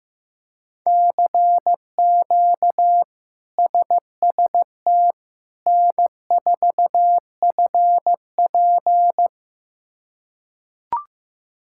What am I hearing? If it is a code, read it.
Morse code, heard as a keyed tone, about 15 wpm, 700 Hz. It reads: CQ SST N4FP
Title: K1USN SST QSO Practice - 15wpm